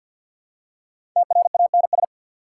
Morse code, heard as a keyed tone, about 54 wpm, 690 Hz. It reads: TRANS
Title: Prefixes and Suffixes 50wpm